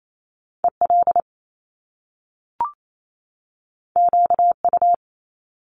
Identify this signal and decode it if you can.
Morse code, heard as a keyed tone, about 28 wpm, 700 Hz. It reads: EL QU
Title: Top Two Letter Combinations in Words 28wpm